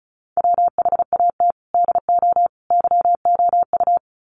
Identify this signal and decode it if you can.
Morse code, heard as a keyed tone, about 34 wpm, 700 Hz. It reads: WHAT DO YOU
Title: Sets of 3 Words 35wpm